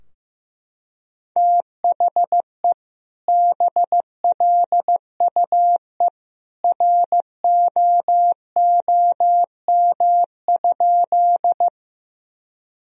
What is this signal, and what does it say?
Morse code, heard as a keyed tone, about 16 wpm, 700 Hz. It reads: THE BLUE ROOM?
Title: Sentences from Top 1000 Words - Review Part 3 - 15wpm